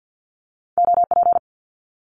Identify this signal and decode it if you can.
Morse code, heard as a keyed tone, about 50 wpm, 700 Hz. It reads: OP
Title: Abbreviations Repeated 3x - 50wpm